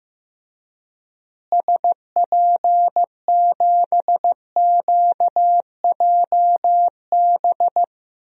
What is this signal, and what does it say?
Morse code, heard as a keyed tone, about 16 wpm, 700 Hz. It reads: SP7QJB
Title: DXCC-100 Call Signs - 6 Characters - Repeated 3x - 15wpm